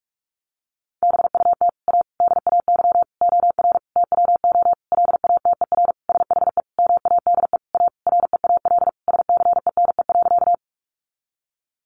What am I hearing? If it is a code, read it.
Morse code, heard as a keyed tone, about 44 wpm, 700 Hz. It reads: BUT A DAY OR TWO LATER SHE MADE A REAL SCENE.
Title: Sentences from Top 1100 Words 45wpm